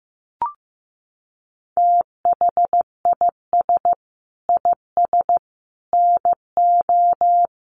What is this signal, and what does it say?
Morse code, heard as a keyed tone, about 15 wpm, 700 Hz. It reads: THIS IS NO
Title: Sentences from Top 700 Words 15wpm